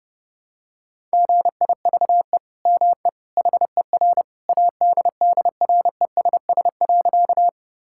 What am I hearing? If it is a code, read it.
Morse code, heard as a keyed tone, about 30 wpm, 700 Hz. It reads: GIVE ME HER ADDRESS.
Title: Sentences from Top 1000 Words - Review Part 6 - 30wpm